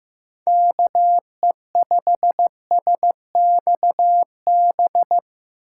Morse code, heard as a keyed tone, about 15 wpm, 700 Hz. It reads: KE5SXB
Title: USA Call Signs - 6 Characters - 3x Character Spacing - 15wpm